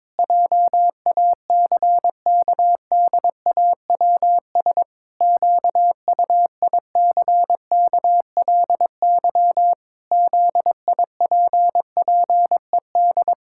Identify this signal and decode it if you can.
Morse code, heard as a keyed tone, about 22 wpm, 700 Hz. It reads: JACKDAWS QUICKLY ZIPPED
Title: Pangram Challenge v2 - 22wpm